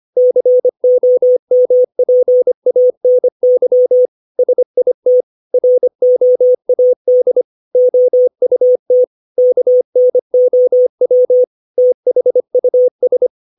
Morse code, heard as a keyed tone, about 25 wpm, 500 Hz. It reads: COMPANY SIT ROAD OUT KNOW THUS
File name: 632 words / 632 Words 25WPM